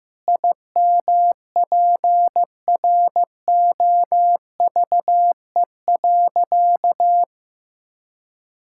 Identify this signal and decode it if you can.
Morse code, heard as a keyed tone, about 15 wpm, 700 Hz. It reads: IMPROVE.
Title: Sentences from Top 1000 Words - Review Part 5 - 15wpm